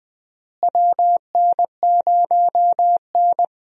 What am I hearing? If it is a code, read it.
Morse code, heard as a keyed tone, about 21 wpm, 700 Hz. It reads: WN0N
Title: USA Call Signs - 4 Characters - Repeated 2x - 20wpm